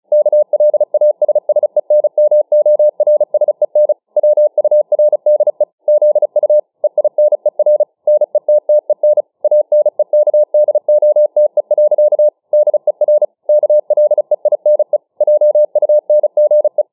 Morse code, heard as a keyed tone, about 36 wpm, 600 Hz. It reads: KLASSENMORSEN WURDE ZU EINER NETTEN ANEKDOTE. DER KLEINE JUNGE